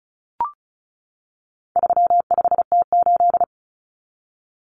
Morse code, heard as a keyed tone, about 34 wpm, 700 Hz. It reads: 35T8
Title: Common Cut Numbers - 4-Digit 35wpm